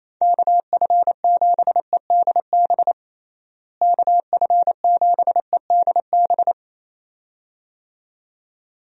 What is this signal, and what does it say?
Morse code, heard as a keyed tone, about 27 wpm, 700 Hz. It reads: KF7EDB KF7EDB
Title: USA Call Signs - 6 Characters - Repeated 2x - 28wpm